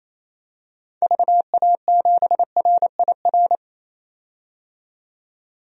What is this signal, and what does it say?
Morse code, heard as a keyed tone, about 27 wpm, 700 Hz. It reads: VA7RIR
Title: DXCC-100 Call Signs - 6 Characters - 28wpm